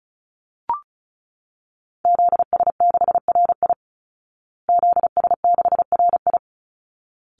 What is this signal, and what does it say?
Morse code, heard as a keyed tone, about 34 wpm, 700 Hz. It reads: ZS6RI ZS6RI
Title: Call Signs Repeated 3x - 35wpm